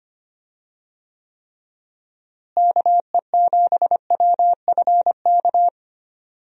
Morse code, heard as a keyed tone, about 25 wpm, 700 Hz. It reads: KE7WFK
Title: USA Call Signs - 6 Characters - Repeated 2x - 25wpm